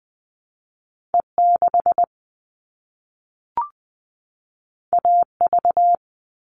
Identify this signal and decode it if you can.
Morse code, heard as a keyed tone, about 20 wpm, 700 Hz. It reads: E6 AV
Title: All Cut Numbers - 2-Digit 20wpm